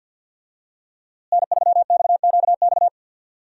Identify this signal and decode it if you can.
Morse code, heard as a keyed tone, about 52 wpm, 700 Hz. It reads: N3XQX
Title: USA Call Signs - 5 Characters - Repeated 2x - 50wpm